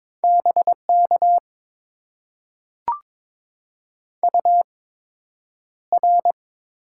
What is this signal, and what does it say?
Morse code, heard as a keyed tone, about 22 wpm, spaced as for 9 wpm, 700 Hz. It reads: BK UR
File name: Top 90 QSO Elements - 8x Character Spacing - 22wpm